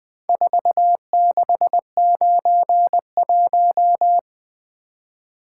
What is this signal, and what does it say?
Morse code, heard as a keyed tone, about 20 wpm, 700 Hz. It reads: 4691
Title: Four Letter-Number Combinations - 20wpm